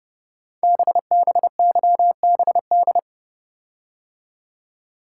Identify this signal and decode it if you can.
Morse code, heard as a keyed tone, about 30 wpm, 700 Hz. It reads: BBYBD
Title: Code Groups - 5 Letters - 30wpm